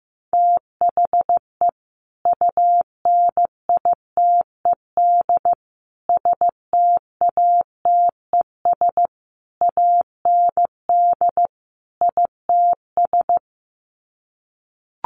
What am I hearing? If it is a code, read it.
Morse code, heard as a keyed tone, about 15 wpm, 700 Hz. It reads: THE UNITED STATES AND ITS
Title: Sets of 5 Words Encore 15wpm